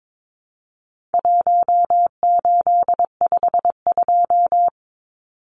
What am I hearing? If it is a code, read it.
Morse code, heard as a keyed tone, about 22 wpm, 700 Hz. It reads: 1852
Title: Numbers - 4-Digit 22wpm